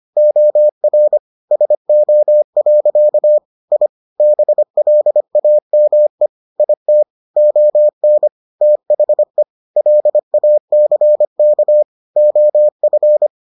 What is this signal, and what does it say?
Morse code, heard as a keyed tone, about 25 wpm, 600 Hz. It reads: OR SO. I BLAME IT ON THE LACK OF